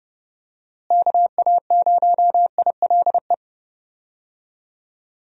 Morse code, heard as a keyed tone, about 32 wpm, 700 Hz. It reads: KA0ILE
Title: USA Call Signs - 6 Characters - Repeated 2x - 30wpm